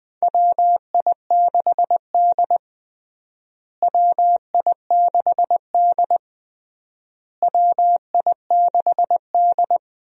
Morse code, heard as a keyed tone, about 20 wpm, 700 Hz. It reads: WI6D WI6D WI6D
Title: USA Call Signs - 4 Characters - Repeated 3x - 20wpm